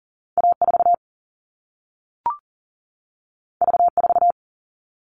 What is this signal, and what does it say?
Morse code, heard as a keyed tone, about 42 wpm, 700 Hz. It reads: A4 V4
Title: All Cut Numbers - 2-Digit 40wpm